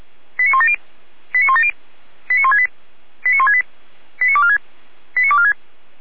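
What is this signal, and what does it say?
Five-tone selective call (ZVEI), heard as quick runs of five tones, tones of about 70 ms. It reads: 89170 89170 89168 89168 89246 89246
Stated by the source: ZVEI five-tone selective call received by SDR, hosted by feuerwehr-orlamuende (fire brigade Orlamünde).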